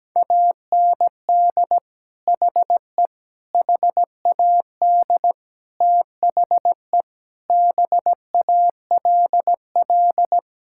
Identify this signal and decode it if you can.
Morse code, heard as a keyed tone, about 17 wpm, 700 Hz. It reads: AND HE HAD THE BALL
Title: Sentences from Top 1000 Words - Review Part 6 - 17wpm